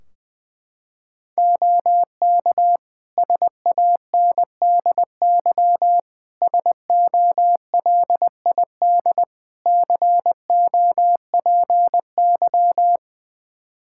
Morse code, heard as a keyed tone, about 21 wpm, 700 Hz. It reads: OK SANDY SOLID COPY
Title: K1USN SST QSO Practice 20wpm